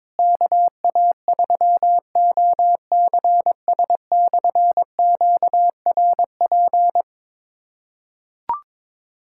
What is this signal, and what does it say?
Morse code, heard as a keyed tone, about 22 wpm, 700 Hz. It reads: KA3OCS/QRP
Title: K1USN SST QSO Practice - 22wpm